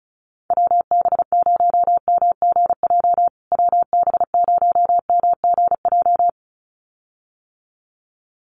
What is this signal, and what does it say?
Morse code, heard as a keyed tone, about 34 wpm, 700 Hz. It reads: WB0MGJ WB0MGJ
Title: USA Call Signs Repeated 2x 35wpm